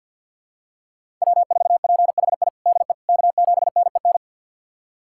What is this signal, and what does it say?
Morse code, heard as a keyed tone, about 52 wpm, 700 Hz. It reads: W4PHI DE K7NEN
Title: USA Callsign DE Callsign - 5 Characters - 50wpm